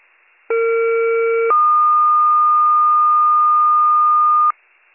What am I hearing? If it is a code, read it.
Two-tone page, heard as a one-second tone, then a three-second tone, about 460 then 1220 Hz. It is heard from 0.5 to 4.5 s.